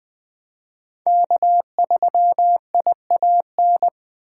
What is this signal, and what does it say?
Morse code, heard as a keyed tone, about 25 wpm, 700 Hz. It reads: K3IAN
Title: Speed Racing - USA 1x3 Call Signs 20wpm